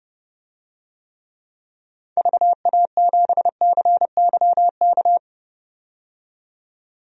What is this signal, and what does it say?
Morse code, heard as a keyed tone, about 30 wpm, 700 Hz. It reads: VA7CYK
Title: DXCC-100 Call Signs - 6 Characters - 30wpm